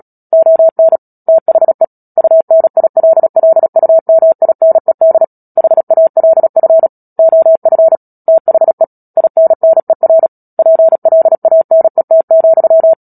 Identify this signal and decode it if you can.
Morse code, heard as a keyed tone, about 37 wpm, 650 Hz. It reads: ON THE UNILLUMINED HALF OF THE INNER PLANET,
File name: Book Two / 10-Chapter 10 - Epilogue - 36 WPM